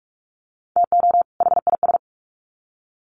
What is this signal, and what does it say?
Morse code, heard as a keyed tone, about 44 wpm, 700 Hz. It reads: TO HIS
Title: Sets of 2 Words 45wpm